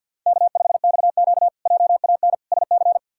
Morse code, heard as a keyed tone, about 46 wpm, 700 Hz. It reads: K5XQ JAN SC
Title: K1USN SST QSO Practice - 50wpm